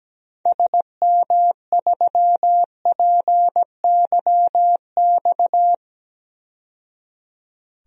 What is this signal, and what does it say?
Morse code, heard as a keyed tone, about 17 wpm, 700 Hz. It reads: SM3PYX
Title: DXCC-100 Call Signs - 6 Characters - 17wpm